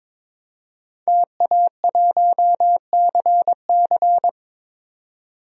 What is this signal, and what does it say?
Morse code, heard as a keyed tone, about 22 wpm, 700 Hz. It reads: TA1CC
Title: DXCC-100 Call Signs - 5 Characters - 22wpm